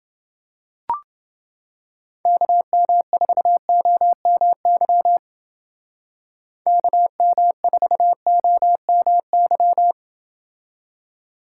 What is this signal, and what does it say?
Morse code, heard as a keyed tone, about 28 wpm, 700 Hz. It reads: KM4OMY KM4OMY